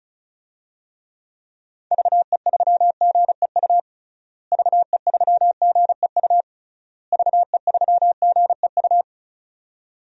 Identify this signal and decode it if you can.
Morse code, heard as a keyed tone, about 36 wpm, 700 Hz. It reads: VE3GEU VE3GEU VE3GEU
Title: DXCC-100 Call Signs - 6 Characters - Repeated 3x - 35wpm